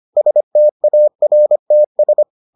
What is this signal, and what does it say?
Morse code, heard as a keyed tone, about 25 wpm, 600 Hz. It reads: STARTS